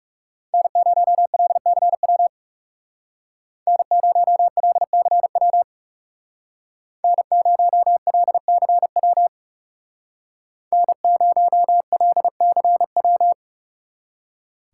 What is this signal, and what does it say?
Morse code, heard as a keyed tone, about 34 wpm, 700 Hz. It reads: N0LCW N0LCW N0LCW N0LCW
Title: Speed Racing - USA Call Signs 30wpm